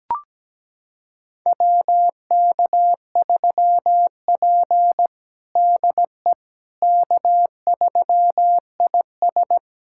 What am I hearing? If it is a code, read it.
Morse code, heard as a keyed tone, about 17 wpm, 700 Hz. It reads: WK3P DE K3IS
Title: USA Callsign DE Callsign - 4 Characters - 17wpm